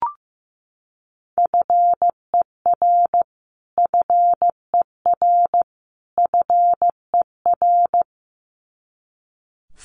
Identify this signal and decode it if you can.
Morse code, heard as a keyed tone, about 15 wpm, 700 Hz. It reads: FER FER FER
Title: Abbreviations Repeated 3x - 15wpm